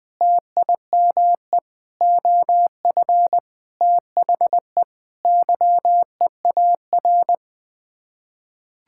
Morse code, heard as a keyed tone, about 21 wpm, 700 Hz. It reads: TIME OF THE YEAR
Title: Sets of 8 Words - 20wpm